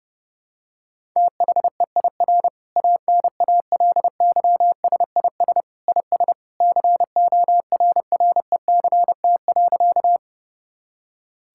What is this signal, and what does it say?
Morse code, heard as a keyed tone, about 30 wpm, 700 Hz. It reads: THEIR ANALYSIS IS CORRECT.
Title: Sentences from Top 1000 Words - Review Part 5 - 30wpm